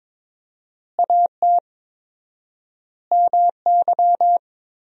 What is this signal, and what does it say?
Morse code, heard as a keyed tone, about 22 wpm, 700 Hz. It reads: AT MY
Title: 4x Word-Spacing - Sentences from Top 200 Words - 22wpm